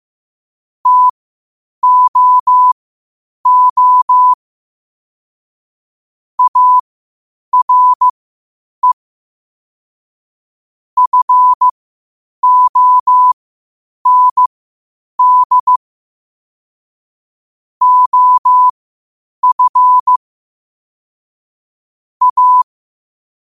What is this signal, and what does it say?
Morse code, heard as a keyed tone, about 15 wpm, spaced as for 8 wpm, 1000 Hz. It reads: TOO ARE FOND OF A
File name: qotdfiles / 8wpm